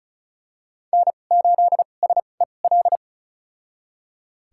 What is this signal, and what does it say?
Morse code, heard as a keyed tone, about 36 wpm, spaced as for 25 wpm, 700 Hz. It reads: N8SEL
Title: USA Call Signs - 5 Characters - 2x Character Spacing - 35wpm